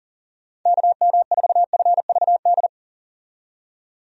Morse code, heard as a keyed tone, about 40 wpm, 700 Hz. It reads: KM4FVD
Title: USA Call Signs - 6 Characters - 40wpm